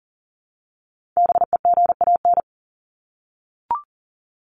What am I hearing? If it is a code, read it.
Morse code, heard as a keyed tone, about 40 wpm, 700 Hz. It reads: BEGAN
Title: Top 300 Words 40wpm